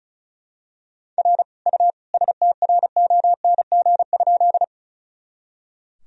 Morse code, heard as a keyed tone, about 36 wpm, 700 Hz. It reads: R U STRONG?
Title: Sentences from Top 500 Words and Abbreviations 35wpm